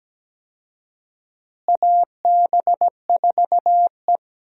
Morse code, heard as a keyed tone, about 17 wpm, 700 Hz. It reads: AB4E
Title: USA Call Signs - 4 Characters - Repeated 3x - 17wpm